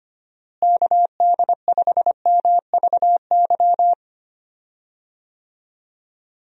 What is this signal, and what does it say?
Morse code, heard as a keyed tone, about 25 wpm, 700 Hz. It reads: KD5MVY
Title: USA Call Signs - 6 Characters - Repeated 2x - 25wpm